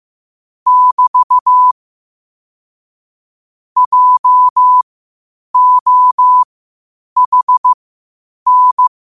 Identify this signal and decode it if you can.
Morse code, heard as a keyed tone, about 15 wpm, spaced as for 8 wpm, 1000 Hz. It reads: <BT> JOHN
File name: qotdfiles / 8wpm